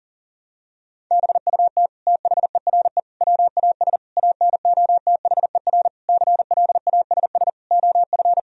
Morse code, heard as a keyed tone, about 40 wpm, 700 Hz. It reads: BUT THERE WAS ANOTHER CLASS OF
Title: Sentences from Top 1000 Words - Review Part 2 - 40wpm